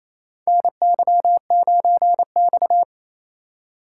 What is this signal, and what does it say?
Morse code, heard as a keyed tone, about 28 wpm, 700 Hz. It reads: NY9X
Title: K1USN SST QSO Practice - 28wpm